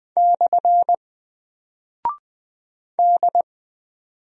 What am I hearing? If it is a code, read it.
Morse code, heard as a keyed tone, about 20 wpm, 700 Hz. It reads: / D
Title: Single Character (Letter:Number:Punctuation:Prosign) 20wpm